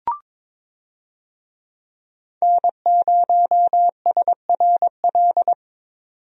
Morse code, heard as a keyed tone, about 22 wpm, 700 Hz. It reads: N0SRL
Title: USA Call Signs - 5 Characters - 22wpm